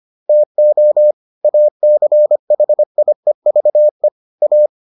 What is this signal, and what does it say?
Morse code, heard as a keyed tone, about 25 wpm, 600 Hz. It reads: TO ACHIEVE A